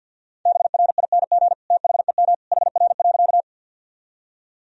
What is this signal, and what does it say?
Morse code, heard as a keyed tone, about 48 wpm, 700 Hz. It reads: BRING THEM HR.
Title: Sentences from Top 500 Words and Abbreviations 50wpm